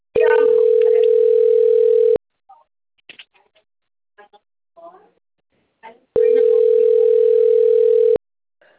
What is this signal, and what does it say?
Ringback tone (North America), 440 + 480 Hz.